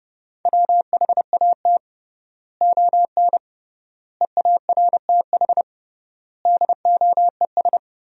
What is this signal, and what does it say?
Morse code, heard as a keyed tone, about 30 wpm, 700 Hz. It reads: WHAT ON EARTH DOES
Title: 3x Word-Spacing - Sentences from Top 200 Words - 30wpm